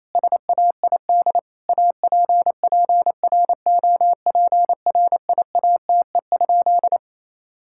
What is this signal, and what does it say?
Morse code, heard as a keyed tone, about 28 wpm, 700 Hz. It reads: SAID APPROPRIATE?
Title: Sentences from Top 1000 Words - Review Part 5 - 28wpm